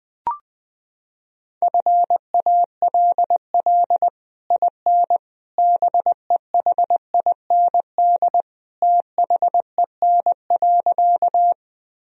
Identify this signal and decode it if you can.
Morse code, heard as a keyed tone, about 20 wpm, 700 Hz. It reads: FALL IN BEHIND THEN.
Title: Sentences from Top 1000 Words - Review Part 2 - 20wpm